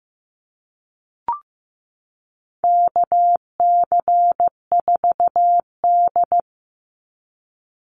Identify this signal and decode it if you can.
Morse code, heard as a keyed tone, about 15 wpm, 700 Hz. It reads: KC4D
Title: Call Signs Repeated 3x - 15wpm